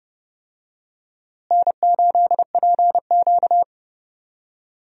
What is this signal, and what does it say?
Morse code, heard as a keyed tone, about 32 wpm, 700 Hz. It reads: N8PQ
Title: USA Call Signs - 4 Characters - Repeated 2x - 30wpm